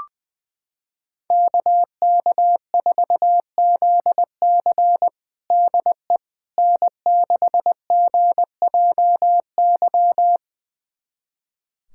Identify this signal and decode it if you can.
Morse code, heard as a keyed tone, about 20 wpm, 700 Hz. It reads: KK4ZC DE N6GJY
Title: USA Callsign DE Callsign - 5 Characters - 20wpm